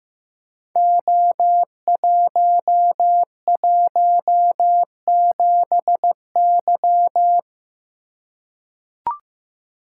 Morse code, heard as a keyed tone, about 15 wpm, 700 Hz. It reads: O117Y
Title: Code Groups - 5 Letter:Numbers - 15wpm